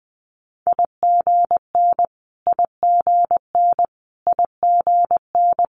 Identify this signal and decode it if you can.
Morse code, heard as a keyed tone, about 20 wpm, 700 Hz. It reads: IGN IGN IGN
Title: Multi-letter Phonemes Repeated 3x 20wpm